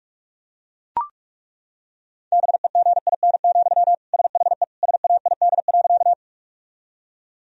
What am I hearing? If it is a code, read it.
Morse code, heard as a keyed tone, about 41 wpm, 700 Hz. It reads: BEGIN, SHE SAID.
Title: Sentences from Top 1000 Words - Review Part 1 - 45wpm